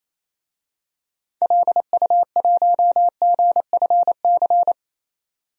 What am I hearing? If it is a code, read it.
Morse code, heard as a keyed tone, about 28 wpm, 700 Hz. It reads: LU1GFC
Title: DXCC-100 Call Signs - 6 Characters - 28wpm